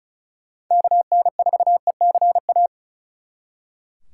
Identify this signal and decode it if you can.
Morse code, heard as a keyed tone, about 45 wpm, 700 Hz. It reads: KN4ECA